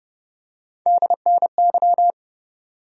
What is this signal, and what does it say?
Morse code, heard as a keyed tone, about 30 wpm, 700 Hz. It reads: DNY
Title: USA Call Sign Postfixes - 3 Letters 30wpm